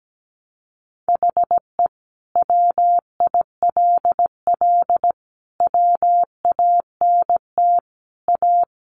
Morse code, heard as a keyed tone, about 17 wpm, 700 Hz. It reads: HE WILL WANT A
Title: Sentences from Top 1300 Words 17wpm